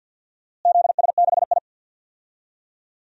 Morse code, heard as a keyed tone, about 46 wpm, 700 Hz. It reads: ZS6I
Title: DXCC-100 Call Signs - 4 Characters - Repeated 3x - 50wpm